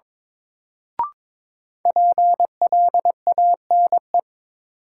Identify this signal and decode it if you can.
Morse code, heard as a keyed tone, about 21 wpm, 700 Hz. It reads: PLANE
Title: Sentences from Top 1000 Words - Review Part 3 - New Word Drill - 22wpm